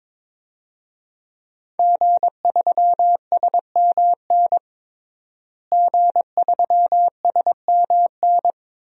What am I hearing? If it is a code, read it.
Morse code, heard as a keyed tone, about 22 wpm, 700 Hz. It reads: G3SMN G3SMN
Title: DXCC-100 Call Signs - 5 Characters - Repeated 3x - 22wpm